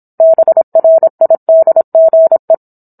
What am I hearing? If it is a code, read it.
Morse code, heard as a keyed tone, about 27 wpm, 650 Hz. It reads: BRIDGE
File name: Book One / 17-Chapter 17 - 26 WPM